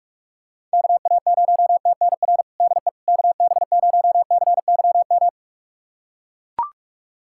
Morse code, heard as a keyed tone, about 43 wpm, 700 Hz. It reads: KA0TNR DE KB0CYM
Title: USA Callsign DE Callsign 45wpm